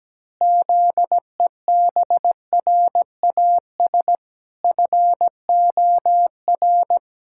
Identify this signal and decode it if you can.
Morse code, heard as a keyed tone, about 17 wpm, 700 Hz. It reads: ZEBRAS FOR
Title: Pangram Challenge v2 - 17wpm